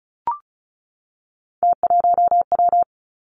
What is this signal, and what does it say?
Morse code, heard as a keyed tone, about 36 wpm, 700 Hz. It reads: T1W
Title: Three Challenging Letter-Number Combinations 35wpm